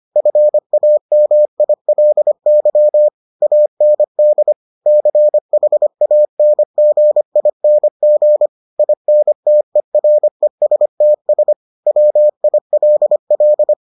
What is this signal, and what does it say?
Morse code, heard as a keyed tone, about 25 wpm, 600 Hz. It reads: FAMILY AND CHANGING INTERESTS WILL